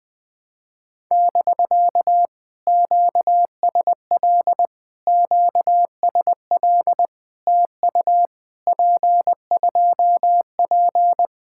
Morse code, heard as a keyed tone, about 20 wpm, 700 Hz. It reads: <BK> QSL QSL TU P2P
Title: POTA QSO Practice - 20wpm